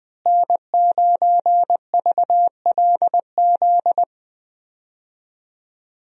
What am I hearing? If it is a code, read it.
Morse code, heard as a keyed tone, about 20 wpm, 700 Hz. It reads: N9VLZ
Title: USA Call Signs - 5 Characters - Repeated 2x - 20wpm